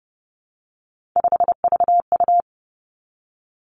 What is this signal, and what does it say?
Morse code, heard as a keyed tone, about 30 wpm, 700 Hz. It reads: 5VU
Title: Three Challenging Letter-Number Combinations 30wpm